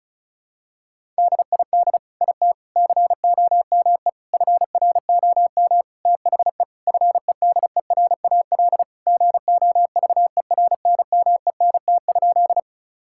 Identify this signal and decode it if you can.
Morse code, heard as a keyed tone, about 36 wpm, 700 Hz. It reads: DID IT COME FROM THE FEDERAL GOVERNMENT?
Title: Sentences from Top 1000 Words - Review Part 5 - 35wpm